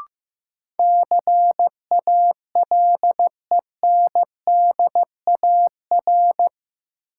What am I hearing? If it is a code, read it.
Morse code, heard as a keyed tone, about 15 wpm, 700 Hz. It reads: CALENDAR
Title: Sentences from Top 1600 Words - New Word Drill - 15wpm